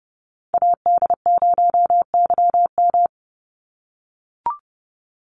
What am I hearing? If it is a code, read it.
Morse code, heard as a keyed tone, about 30 wpm, 700 Hz. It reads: AD0YM
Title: NA SOTA Chaser Callsigns 30wpm